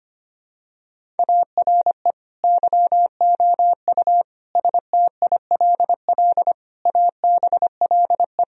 Morse code, heard as a keyed tone, about 25 wpm, 700 Hz. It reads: ARE YOU STILL ABLE
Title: Sentences from Top 1000 Words - Review Part 2 - 25wpm